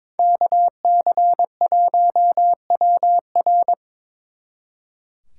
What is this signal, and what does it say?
Morse code, heard as a keyed tone, about 24 wpm, 700 Hz. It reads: KC1WR
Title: Speed Racing - USA 2x2 Call Signs 22wpm